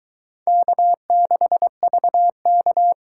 Morse code, heard as a keyed tone, about 22 wpm, 700 Hz. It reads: K6VK